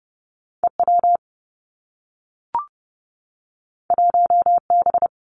Morse code, heard as a keyed tone, about 30 wpm, 700 Hz. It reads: E W 1 B
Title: Two Challenging Letter-Number Combinations 30wpm